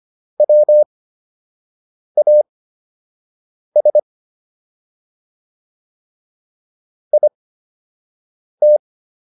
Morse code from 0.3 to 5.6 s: WAS

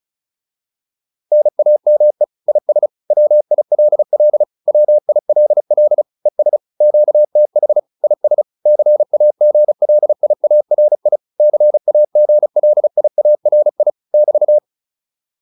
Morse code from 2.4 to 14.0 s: IS WILL WILL ES QTH IS CAGLIARI CAGLIARI

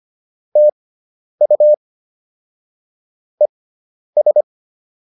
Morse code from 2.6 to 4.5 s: ES

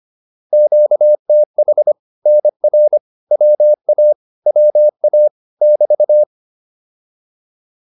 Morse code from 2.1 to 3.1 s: NR

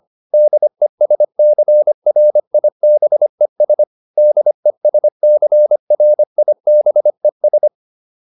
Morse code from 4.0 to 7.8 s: DESCRIBES

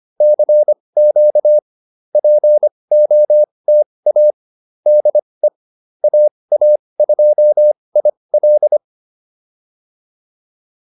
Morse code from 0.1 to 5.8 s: CQ POTA DE

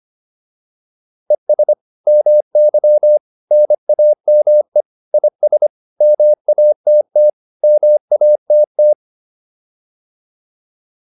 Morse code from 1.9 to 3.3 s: MY

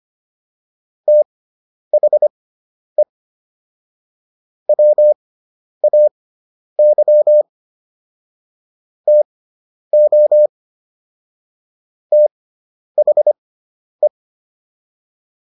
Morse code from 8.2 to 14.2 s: TO THE

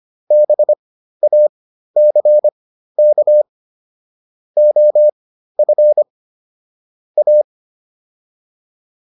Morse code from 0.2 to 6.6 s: BACK OF

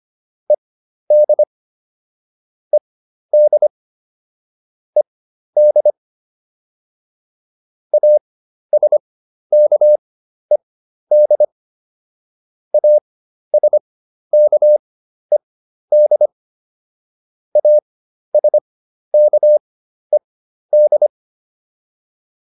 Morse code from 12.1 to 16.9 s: ASKED

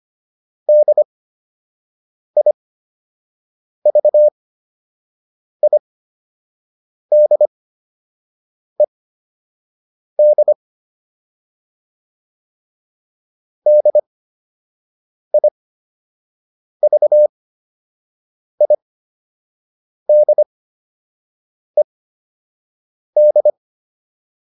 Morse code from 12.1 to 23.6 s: DIVIDED

